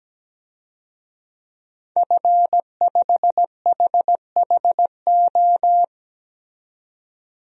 Morse code, heard as a keyed tone, about 17 wpm, 700 Hz. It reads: F5HHO
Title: DXCC-100 Call Signs - 5 Characters - 17wpm